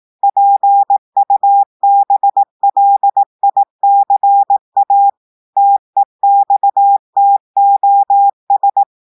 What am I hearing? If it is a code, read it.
Morse code, heard as a keyed tone, about 18 wpm, 800 Hz. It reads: PUBLICA TEXTOS
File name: audios / portugais / portugais texte en clair 1 vitesse 18 mots minutes